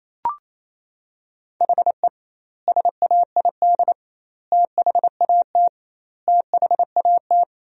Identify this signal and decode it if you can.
Morse code, heard as a keyed tone, about 28 wpm, 700 Hz. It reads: HE SAID THAT THAT
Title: 2x Word-Spacing - Sentences from Top 200 Words - 28wpm